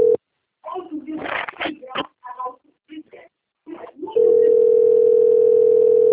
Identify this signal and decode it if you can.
Ringback tone (North America), 440 + 480 Hz, with speech.
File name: out- - - - - .13380